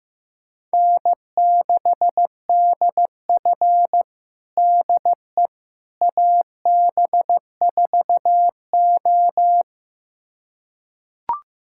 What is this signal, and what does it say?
Morse code, heard as a keyed tone, about 15 wpm, 700 Hz. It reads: N6DF DE AB4O
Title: USA Callsign DE Callsign - 4 Characters - 15wpm